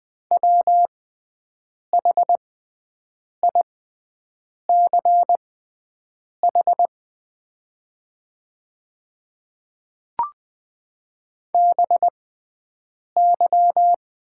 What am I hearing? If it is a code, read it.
Morse code, heard as a keyed tone, about 20 wpm, spaced as for 7 wpm, 700 Hz. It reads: WHICH BY
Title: 6x Character Spacing - Top 100 Words - 20wpm